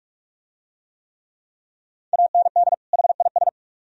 Morse code, heard as a keyed tone, about 43 wpm, 700 Hz. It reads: AND HIS